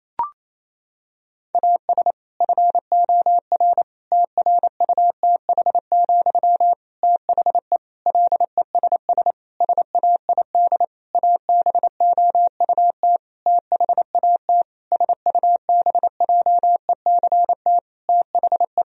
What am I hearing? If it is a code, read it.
Morse code, heard as a keyed tone, about 28 wpm, 700 Hz. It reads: AS FOR TRUTH, THE LESS SAID ABOUT THAT SUBJECT THE
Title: Sentences from Top 1000 Words - Review Part 2 - 28wpm